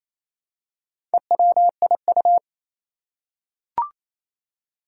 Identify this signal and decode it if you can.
Morse code, heard as a keyed tone, about 28 wpm, 700 Hz. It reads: EWIU
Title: Four Challenging Letter-Number Combinations 28wpm